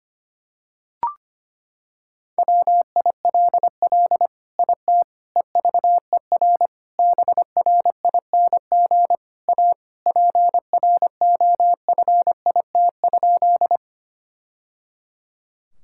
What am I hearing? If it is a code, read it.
Morse code, heard as a keyed tone, about 25 wpm, 700 Hz. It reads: WILL IT EVER BRING A PROFIT?
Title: Sentences from Top 1000 Words - Review Part 5 - 25wpm